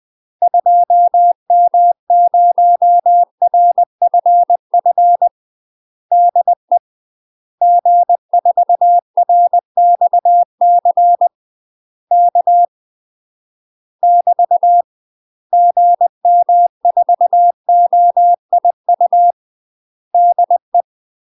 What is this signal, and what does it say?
Morse code, heard as a keyed tone, about 20 wpm, 700 Hz. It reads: 2M0RFF DE G4RXC K <BT> GM4OIU DE
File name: UK MASTER QSOs TRAINING / 20WPM / UK MASTER QSOs 0048